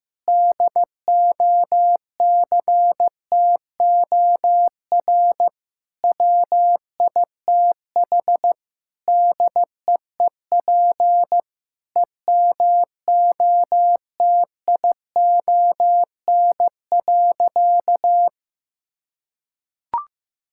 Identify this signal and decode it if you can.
Morse code, heard as a keyed tone, about 15 wpm, 700 Hz. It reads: DOCTOR WITH DEEP EMOTION.
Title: Sentences from Top 1600 Words 15wpm